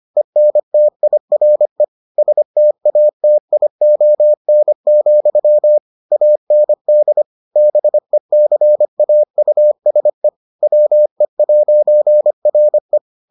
Morse code, heard as a keyed tone, about 25 wpm, 600 Hz. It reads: ENTIRE STATION, AND BECAUSE WE'RE